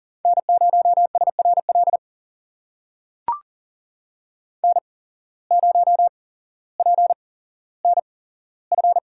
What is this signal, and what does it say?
Morse code, heard as a keyed tone, about 40 wpm, spaced as for 17 wpm, 700 Hz. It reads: N0SRL N0PNF
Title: USA Call Signs - 5 Characters - 8x Character Spacing - 40wpm